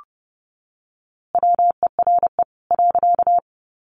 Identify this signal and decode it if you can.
Morse code, heard as a keyed tone, about 30 wpm, 700 Hz. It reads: WERE .
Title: A Sound Of Thunder 30wpm